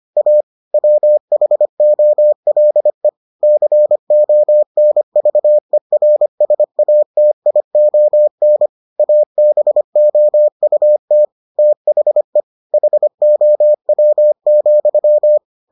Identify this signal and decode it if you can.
Morse code, heard as a keyed tone, about 25 wpm, 600 Hz. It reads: A WHOLE CONVERSATION ABOUT THE HOW,